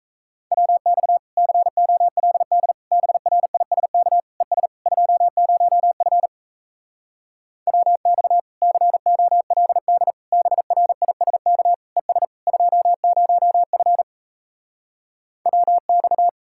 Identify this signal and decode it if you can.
Morse code, heard as a keyed tone, about 37 wpm, 700 Hz. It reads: WX COLD BRISK ES 20F WX COLD BRISK ES 20F WX